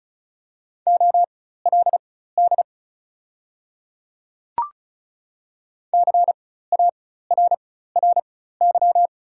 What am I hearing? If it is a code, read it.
Morse code, heard as a keyed tone, about 36 wpm, spaced as for 16 wpm, 700 Hz. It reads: OLD CARRY
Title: 4x Character Spacing - Top 300 Words - 35wpm